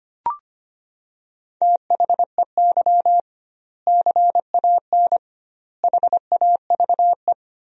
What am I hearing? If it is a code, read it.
Morse code, heard as a keyed tone, about 25 wpm, 700 Hz. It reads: THEY CAN HAVE
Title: 2x Word-Spacing - Sentences from Top 400 Words - 25wpm